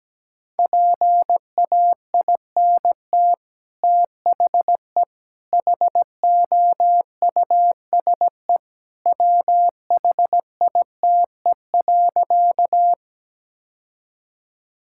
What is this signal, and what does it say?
Morse code, heard as a keyed tone, about 17 wpm, 700 Hz. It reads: PAINT THE HOUSE WHITE.
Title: Sentences from Top 1000 Words - Review Part 6 - 17wpm